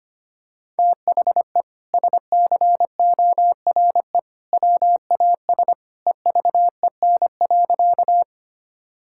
Morse code, heard as a keyed tone, about 25 wpm, 700 Hz. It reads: THE SCORE WAS EVEN.
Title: Sentences from Top 1600 Words 25wpm